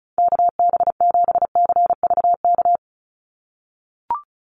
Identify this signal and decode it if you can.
Morse code, heard as a keyed tone, about 34 wpm, 700 Hz. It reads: KB7CVK
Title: USA Call Signs 35wpm